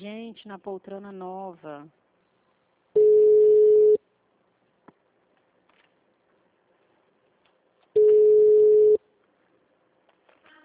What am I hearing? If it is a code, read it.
Ringback tone (Europe, 425 Hz), with speech.